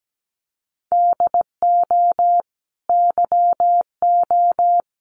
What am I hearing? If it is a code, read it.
Morse code, heard as a keyed tone, about 17 wpm, 700 Hz. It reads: DO YO
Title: Sentences from Top 1500 Words 17wpm